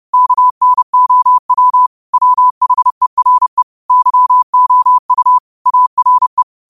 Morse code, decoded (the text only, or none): KNOW WHERE YOU ARE